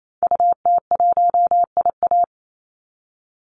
Morse code, heard as a keyed tone, about 28 wpm, 700 Hz. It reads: UT1IA
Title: Call Signs Repeated 3x - 28wpm